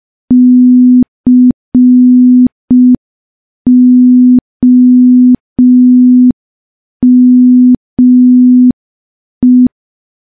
Morse code, heard as a keyed tone, about 5 wpm, 250 Hz. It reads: COME
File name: morse / morse-training-05WPM